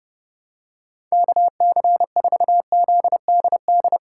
Morse code, heard as a keyed tone, about 30 wpm, 700 Hz. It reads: KC4ZDD
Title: USA Call Signs - 3x Character Spacing - 30wpm